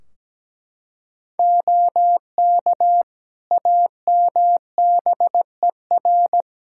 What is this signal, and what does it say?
Morse code, heard as a keyed tone, about 17 wpm, 700 Hz. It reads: OK AMBER
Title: Practice QSO - Exchange 3 - More Info - 17wpm